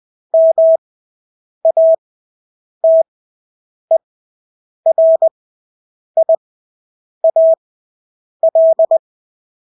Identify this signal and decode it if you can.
Morse code, heard as a keyed tone, about 20 wpm, spaced as for 8 wpm, 650 Hz. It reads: MATERIAL
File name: Engelske ord 20/8 wpm